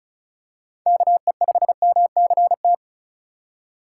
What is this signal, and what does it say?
Morse code, heard as a keyed tone, about 36 wpm, 700 Hz. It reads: KE5MCT
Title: USA Call Signs - 6 Characters - 4x Character Spacing - 35wpm